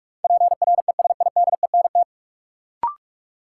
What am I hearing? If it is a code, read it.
Morse code, heard as a keyed tone, about 44 wpm, 700 Hz. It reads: PRESIDENT
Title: Sets of 2 Words - New Word Drill - 45wpm